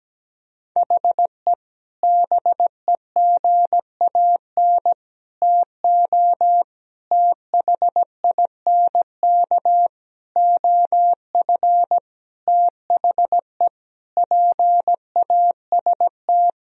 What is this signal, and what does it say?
Morse code, heard as a keyed tone, about 17 wpm, 700 Hz. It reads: HE BEGAN TO THINK OF THE PAST
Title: Sentences from Top 1000 Words - Review Part 3 - 17wpm